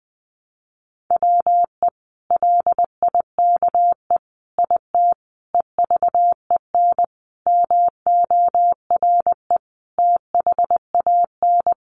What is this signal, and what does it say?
Morse code, heard as a keyed tone, about 20 wpm, 700 Hz. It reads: WE LIKE IT EVEN MORE THAN
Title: Sentences from Top 1100 Words 20wpm